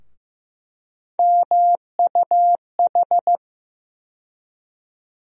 Morse code, heard as a keyed tone, about 15 wpm, 700 Hz. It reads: MUH
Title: USA Call Sign Postfixes - 3 Letters 15wpm